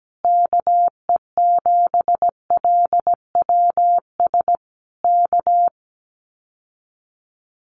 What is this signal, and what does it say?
Morse code, heard as a keyed tone, about 17 wpm, 700 Hz. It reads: KE7LWS K
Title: Common QSO Phrases Encore 17wpm